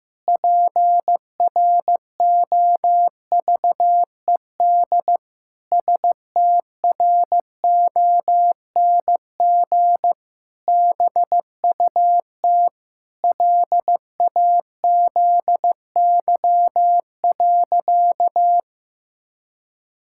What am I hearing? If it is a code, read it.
Morse code, heard as a keyed tone, about 15 wpm, 700 Hz. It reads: PROVED STRONG BUT LAZY.
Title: Pangram Challenge v2 - 15wpm